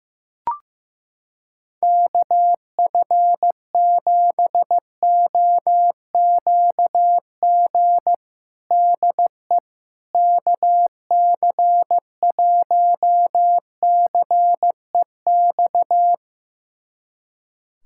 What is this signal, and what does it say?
Morse code, heard as a keyed tone, about 15 wpm, 700 Hz. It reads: KF7OQG DE KC1CEX
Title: USA Callsign DE Callsign - 6 Characters - 15wpm